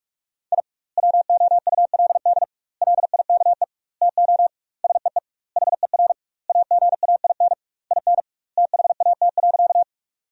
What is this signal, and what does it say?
Morse code, heard as a keyed tone, about 41 wpm, 700 Hz. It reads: I WOULD LIKE TO SEE HER AGAIN IN THAT.
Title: 2x Word-Spacing - Sentences from Top 200 Words - 45wpm